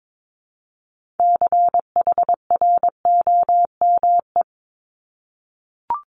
Morse code, heard as a keyed tone, about 22 wpm, 700 Hz. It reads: CHROME
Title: Multi-letter Phonemes 22wpm